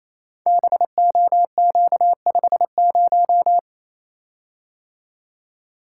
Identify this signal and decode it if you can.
Morse code, heard as a keyed tone, about 29 wpm, 700 Hz. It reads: BOQ50
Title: Code Groups - 5 Letter:Numbers - 28wpm